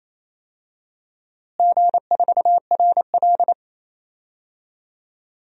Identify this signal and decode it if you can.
Morse code, heard as a keyed tone, about 28 wpm, 700 Hz. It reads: G4RL
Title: DXCC-100 Call Signs - 4 Characters - Repeated 3x - 28wpm